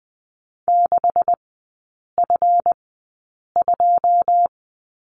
Morse code, heard as a keyed tone, about 20 wpm, spaced as for 8 wpm, 690 Hz. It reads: 6F2
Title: 145 Lesson 24e - Letter y - Three Characters 20wpm